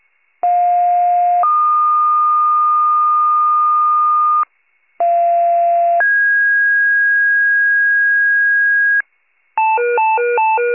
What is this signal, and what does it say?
Two-tone page, heard as a one-second tone, then a three-second tone, about 710 then 1220 Hz, about 690 then 1650 Hz, with noise. It is heard from 0.4 to 9.0 s.